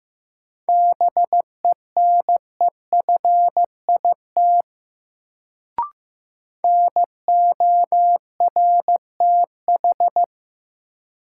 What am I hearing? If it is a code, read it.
Morse code, heard as a keyed tone, about 15 wpm, 700 Hz. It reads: BENEFIT NORTH
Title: Sets of 5 Words Encore - New Word Drill - 15wpm